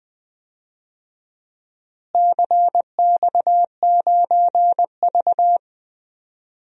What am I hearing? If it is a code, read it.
Morse code, heard as a keyed tone, about 21 wpm, 700 Hz. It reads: CX9V
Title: DXCC-100 Call Signs - 4 Characters - Repeated 3x - 20wpm